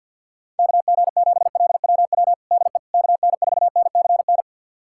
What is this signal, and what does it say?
Morse code, heard as a keyed tone, about 52 wpm, 700 Hz. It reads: KG7LWW DE KN4NCN
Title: USA Callsign DE Callsign 50wpm